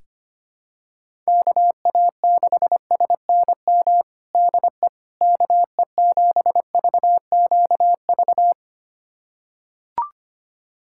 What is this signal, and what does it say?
Morse code, heard as a keyed tone, about 25 wpm, 700 Hz. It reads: KA6SNM DE KE7VQV
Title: USA Callsign DE Callsign - 6 Characters - 25wpm